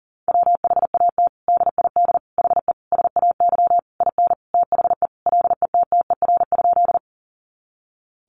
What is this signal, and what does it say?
Morse code, heard as a keyed tone, about 40 wpm, 700 Hz. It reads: WHAT DID HE SAY IN THE LETTER?
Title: Sentences from Top 300 Words 40wpm